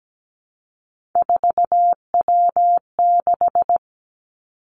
Morse code, heard as a keyed tone, about 17 wpm, 700 Hz. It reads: 4W6
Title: Three Challenging Letter-Number Combinations 17wpm